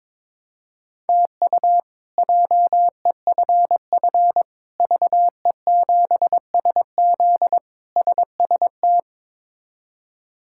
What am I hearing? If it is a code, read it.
Morse code, heard as a keyed tone, about 23 wpm, 700 Hz. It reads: TU JEFF VE7SZ SST
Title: K1USN SST QSO Practice - 22wpm